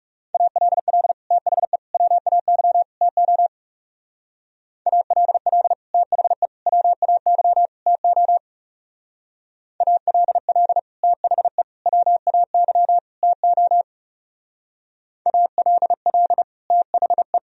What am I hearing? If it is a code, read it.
Morse code, heard as a keyed tone, about 36 wpm, 700 Hz. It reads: ALL THE WAY TO ALL THE WAY TO ALL THE WAY TO ALL THE
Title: Speed-Racing - Sets of 4 Words 30wpm